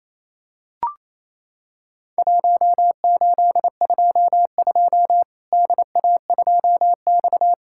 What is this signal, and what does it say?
Morse code, heard as a keyed tone, about 28 wpm, 700 Hz. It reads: 1822 DA2X
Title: Worked All Europe DX Contest - QTC Exchange - 28wpm